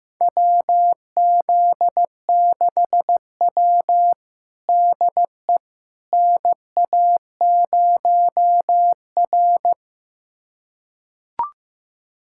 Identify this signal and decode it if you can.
Morse code, heard as a keyed tone, about 15 wpm, 700 Hz. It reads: WZ6W DE NA0R
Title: USA Callsign DE Callsign - 4 Characters - 15wpm